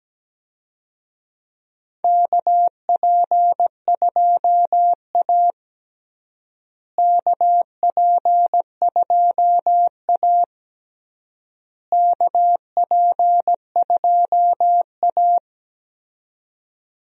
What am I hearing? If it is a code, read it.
Morse code, heard as a keyed tone, about 17 wpm, 700 Hz. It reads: KP2A KP2A KP2A
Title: DXCC-100 Call Signs - 4 Characters - Repeated 3x - 17wpm